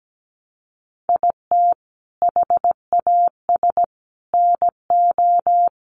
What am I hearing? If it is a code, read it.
Morse code, heard as a keyed tone, about 17 wpm, 700 Hz. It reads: IT HAS NO
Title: Sentences from Top 1200 Words 17wpm